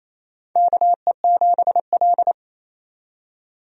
Morse code, heard as a keyed tone, about 28 wpm, 700 Hz. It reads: KE7L
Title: USA Call Signs - 4 Characters - 28wpm